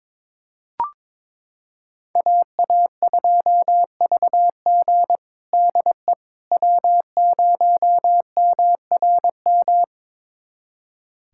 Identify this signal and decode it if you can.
Morse code, heard as a keyed tone, about 21 wpm, 700 Hz. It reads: AA2VG DE W0MRM
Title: USA Callsign DE Callsign - 5 Characters - 22wpm